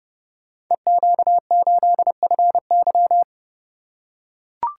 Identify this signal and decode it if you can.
Morse code, heard as a keyed tone, about 30 wpm, 700 Hz. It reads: EQ8FY
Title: Code Groups - 5 Letter:Numbers - 30wpm